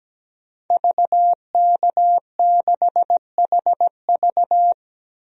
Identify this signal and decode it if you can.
Morse code, heard as a keyed tone, about 17 wpm, 700 Hz. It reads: VK6HV
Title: DXCC-100 Call Signs - 5 Characters - 17wpm